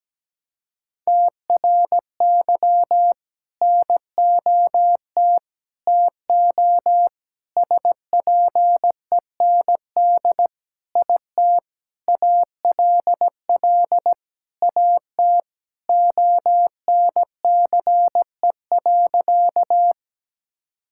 Morse code, heard as a keyed tone, about 17 wpm, 700 Hz. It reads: TRY NOT TO SPEND IT ALL AT ONCE.
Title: Sentences from Top 1000 Words - Review Part 5 - 17wpm